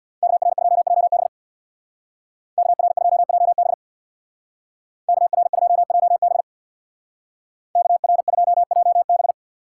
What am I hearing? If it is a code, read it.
Morse code, heard as a keyed tone, about 59 wpm, 700 Hz. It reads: KR2JB KR2JB KR2JB KR2JB
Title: Speed Racing - USA 2x2 Call Signs 50wpm